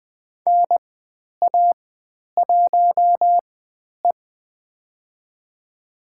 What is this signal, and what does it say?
Morse code, heard as a keyed tone, about 20 wpm, spaced as for 10 wpm, 700 Hz. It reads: NA1E
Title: 095 Lesson 16g - Letter c - Call Signs 20wpm